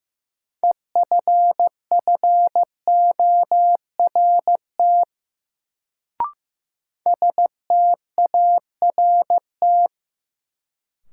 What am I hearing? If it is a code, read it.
Morse code, heard as a keyed tone, about 15 wpm, 700 Hz. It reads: EFFORT START
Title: Sets of 4 Words - New Word Drill - 15wpm